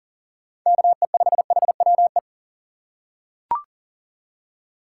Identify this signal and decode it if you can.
Morse code, heard as a keyed tone, about 38 wpm, 700 Hz. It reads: KE5HWE
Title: USA Call Signs - 6 Characters - 4x Character Spacing - 40wpm